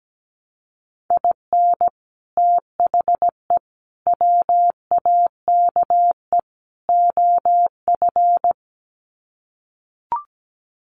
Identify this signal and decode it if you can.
Morse code, heard as a keyed tone, about 17 wpm, 700 Hz. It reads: IN THE WAKE OF
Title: Sets of 4 Words 17wpm